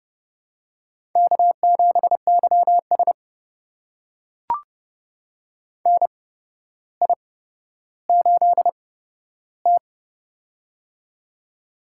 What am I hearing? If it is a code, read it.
Morse code, heard as a keyed tone, about 30 wpm, spaced as for 13 wpm, 700 Hz. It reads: K7YS NI8T
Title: USA Call Signs - 4 Characters - 8x Character Spacing - 30wpm